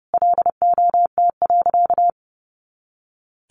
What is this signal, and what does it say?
Morse code, heard as a keyed tone, about 30 wpm, 700 Hz. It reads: LOT.
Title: Easy Phrases 30wpm